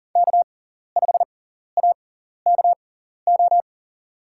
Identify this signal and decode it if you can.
Morse code, heard as a keyed tone, about 39 wpm, spaced as for 14 wpm, 700 Hz. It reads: K5AKO
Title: USA Call Signs - 5 Characters - 6x Character Spacing - 40wpm